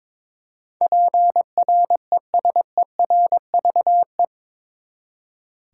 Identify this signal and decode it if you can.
Morse code, heard as a keyed tone, about 22 wpm, 700 Hz. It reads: PRESERVE